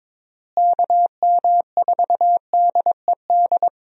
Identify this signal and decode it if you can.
Morse code, heard as a keyed tone, about 22 wpm, 700 Hz. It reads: KM4DED
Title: USA Call Signs - 6 Characters - Repeated 3x - 22wpm